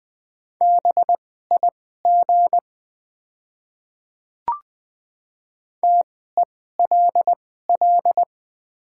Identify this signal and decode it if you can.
Morse code, heard as a keyed tone, about 20 wpm, spaced as for 15 wpm, 690 Hz. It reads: BIG TELL
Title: 2x Character Spacing - Top 200 Words - 20wpm